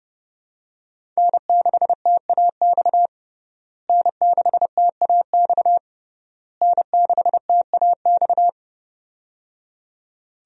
Morse code, heard as a keyed tone, about 30 wpm, 700 Hz. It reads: N6TAX N6TAX N6TAX
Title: USA Call Signs - 5 Characters - Repeated 3x - 30wpm